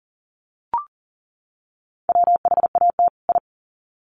Morse code, heard as a keyed tone, about 40 wpm, 700 Hz. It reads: WHAT I
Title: Sets of 2 Words 40wpm